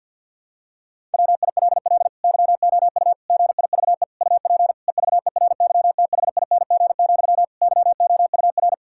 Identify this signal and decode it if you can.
Morse code, heard as a keyed tone, about 50 wpm, 700 Hz. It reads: WILL YOU GIVE UP EVERYTHING, YOUR
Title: Sentences from Top 1000 Words - Review Part 5 - 50wpm